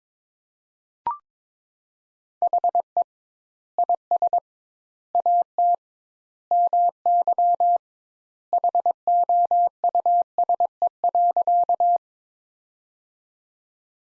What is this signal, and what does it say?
Morse code, heard as a keyed tone, about 22 wpm, 700 Hz. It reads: HE IS AT MY HOUSE.
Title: Sentences Top 200 Words - 22wpm